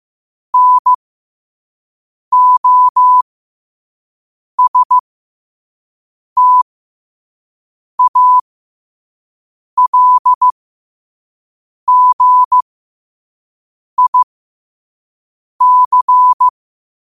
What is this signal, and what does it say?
Morse code, heard as a keyed tone, about 15 wpm, spaced as for 5 wpm, 1000 Hz. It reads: NOSTALGIC
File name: qotdfiles / 5wpm